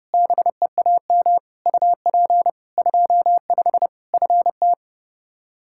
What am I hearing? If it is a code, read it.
Morse code, heard as a keyed tone, about 30 wpm, 700 Hz. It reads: BEAM UP 25 FT
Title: Practice QSO - Exchange 2 - Info - 30wpm